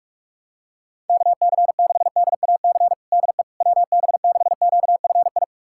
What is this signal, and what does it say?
Morse code, heard as a keyed tone, about 42 wpm, 700 Hz. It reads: KC6DAC DE WB6QFI
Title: USA Callsign DE Callsign - 6 Characters - 45wpm